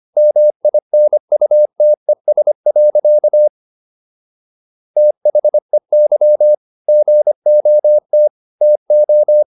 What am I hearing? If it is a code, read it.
Morse code, heard as a keyed tone, about 25 wpm, 600 Hz. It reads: MINUTES. THEY GOT TO